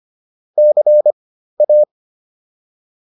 Morse code, heard as a keyed tone, about 25 wpm, spaced as for 13 wpm, 600 Hz. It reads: CA